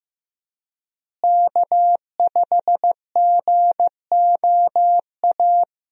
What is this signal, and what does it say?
Morse code, heard as a keyed tone, about 15 wpm, 700 Hz. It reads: K5GOA
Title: USA Call Signs - 5 Characters - Repeated 3x - 15wpm